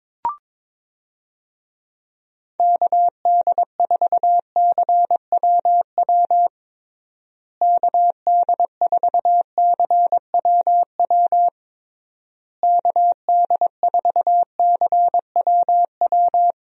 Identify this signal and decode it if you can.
Morse code, heard as a keyed tone, about 22 wpm, 700 Hz. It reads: KD4CWW KD4CWW KD4CWW
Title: USA Call Signs - 6 Characters - Repeated 3x - 22wpm